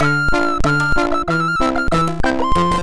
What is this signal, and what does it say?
There is no code.